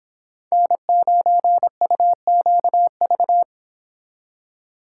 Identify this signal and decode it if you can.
Morse code, heard as a keyed tone, about 26 wpm, 700 Hz. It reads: N9UQV
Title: Speed Racing - USA 1x3 Call Signs 22wpm